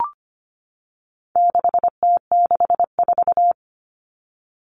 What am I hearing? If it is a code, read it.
Morse code, heard as a keyed tone, about 25 wpm, 700 Hz. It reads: 6T64
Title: All Cut Numbers - 4-Digit 25wpm